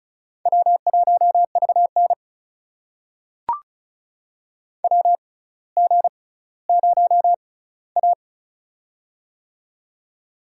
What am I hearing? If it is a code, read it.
Morse code, heard as a keyed tone, about 36 wpm, spaced as for 18 wpm, 700 Hz. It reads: W1VN WG0A
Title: USA Call Signs - 4 Characters - 6x Character Spacing - 35wpm